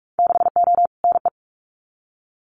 Morse code, heard as a keyed tone, about 44 wpm, 700 Hz. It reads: 6O NE
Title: Winter Field Day Exchanges Encore 45wpm